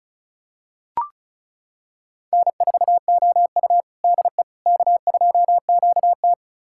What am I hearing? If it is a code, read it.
Morse code, heard as a keyed tone, about 36 wpm, 700 Hz. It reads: N4OU DE K2QT
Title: USA Callsign DE Callsign - 4 Characters - 35wpm